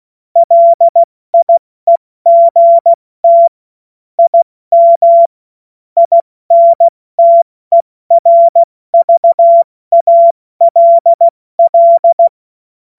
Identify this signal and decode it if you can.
Morse code, heard as a keyed tone, about 16 wpm, spaced as for 14 wpm, 680 Hz. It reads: LIEGT IM INTERVALL
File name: Morsekurs / Witze / Witz 21. Dezember 2021, 70 F80